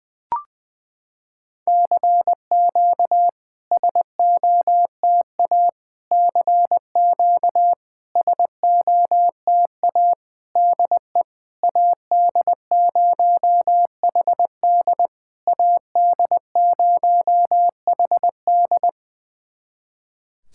Morse code, heard as a keyed tone, about 21 wpm, 700 Hz. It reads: CQ SOTA CQ SOTA DE AD0HD AD0HD
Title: SOTA QSO Practice 20wpm